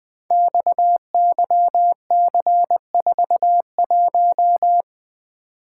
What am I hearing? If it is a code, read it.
Morse code, heard as a keyed tone, about 21 wpm, 700 Hz. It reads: XYC41
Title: Code Groups - 5 Letter:Numbers - 20wpm